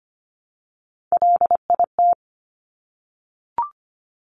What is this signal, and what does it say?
Morse code, heard as a keyed tone, about 25 wpm, 700 Hz. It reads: LIT
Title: Top Three Letter Combinations in Words 25wpm